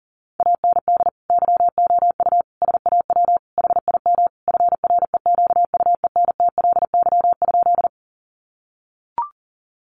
Morse code, heard as a keyed tone, about 40 wpm, 700 Hz. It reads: AND YOU SAW HIM FREQUENTLY?
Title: Sentences from Top 1300 Words 40wpm